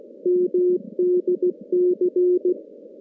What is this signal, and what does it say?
Morse code, heard as a keyed tone, about 16 wpm, 410 Hz. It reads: MDC
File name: contests / CW-SS-k5tr- -2100